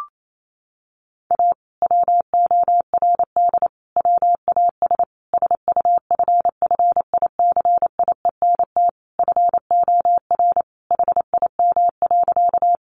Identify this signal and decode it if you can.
Morse code, heard as a keyed tone, about 28 wpm, 700 Hz. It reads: A WORD WAS SUFFICIENT FOR HIM.
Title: Sentences from Top 1400 Words 28wpm